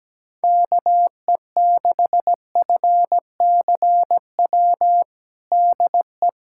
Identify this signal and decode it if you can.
Morse code, heard as a keyed tone, about 17 wpm, 700 Hz. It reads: KE6FCW DE
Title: USA Callsign DE Callsign 17wpm